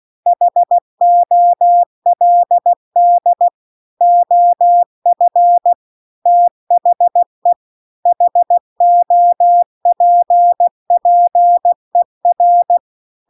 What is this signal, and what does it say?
Morse code, heard as a keyed tone, about 16 wpm, 700 Hz. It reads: HOLD OF THE HOPPER